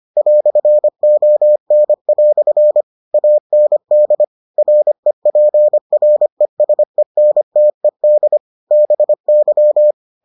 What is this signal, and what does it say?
Morse code, heard as a keyed tone, about 25 wpm, 600 Hz. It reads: "ON" AND REPRESENTED BY